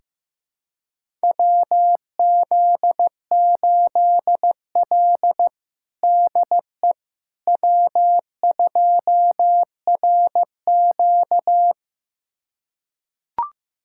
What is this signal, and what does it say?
Morse code, heard as a keyed tone, about 15 wpm, 700 Hz. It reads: WZ8L DE W2RQ
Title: USA Callsign DE Callsign - 4 Characters - 15wpm